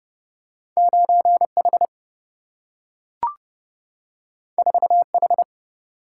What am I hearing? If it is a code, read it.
Morse code, heard as a keyed tone, about 30 wpm, 700 Hz. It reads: 9 H 4 H
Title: Single-Letter Single-Number 30wpm